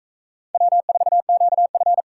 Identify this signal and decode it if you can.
Morse code, heard as a keyed tone, about 41 wpm, 700 Hz. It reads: W4QF